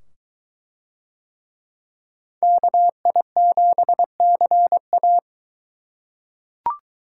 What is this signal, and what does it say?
Morse code, heard as a keyed tone, about 21 wpm, 700 Hz. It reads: KI7CA